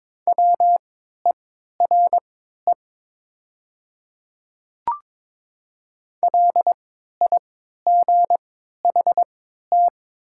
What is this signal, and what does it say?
Morse code, heard as a keyed tone, about 22 wpm, spaced as for 12 wpm, 700 Hz. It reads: WERE LIGHT
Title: 3x Character Spacing - Top 200 Words - 22wpm